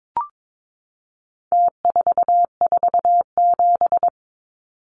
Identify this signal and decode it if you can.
Morse code, heard as a keyed tone, about 22 wpm, 700 Hz. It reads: T447
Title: Numbers - Contest Style - 4-Digit 22wpm